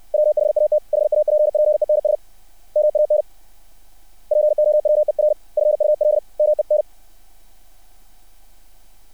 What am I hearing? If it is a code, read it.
Morse code, heard as a keyed tone, about 60 wpm, 600 Hz. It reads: WHAT HAPPENS MAN POWERFUL MEN
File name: archived / CRSnet